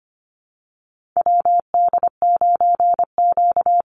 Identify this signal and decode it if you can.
Morse code, heard as a keyed tone, about 25 wpm, 700 Hz. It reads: WD9Q
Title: Call Signs - 25wpm